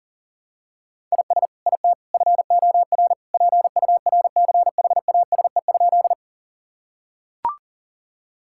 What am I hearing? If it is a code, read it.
Morse code, heard as a keyed tone, about 40 wpm, 700 Hz. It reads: IS IT FOR PURCHASE?
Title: Sentences from Top 1000 Words - Review Part 5 - 40wpm